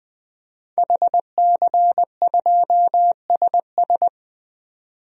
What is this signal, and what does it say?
Morse code, heard as a keyed tone, about 20 wpm, 700 Hz. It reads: HC2SS
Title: DXCC-100 Call Signs - 5 Characters - 20wpm